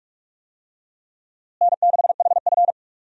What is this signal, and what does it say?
Morse code, heard as a keyed tone, about 45 wpm, 700 Hz. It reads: N6HF
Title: USA Call Signs - 4 Characters - Repeated 2x - 45wpm